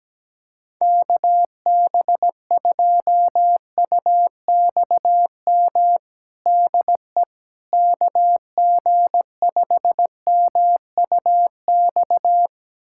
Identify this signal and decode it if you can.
Morse code, heard as a keyed tone, about 17 wpm, 700 Hz. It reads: KB2UXM DE KG5MUX
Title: USA Callsign DE Callsign - 6 Characters - 17wpm